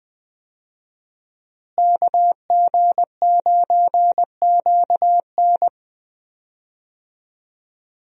Morse code, heard as a keyed tone, about 20 wpm, 700 Hz. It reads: KG9QN
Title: USA Call Signs - 5 Characters - Repeated 2x - 20wpm